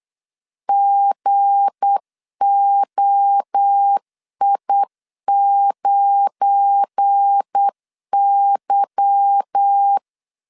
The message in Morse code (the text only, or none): GOI9Y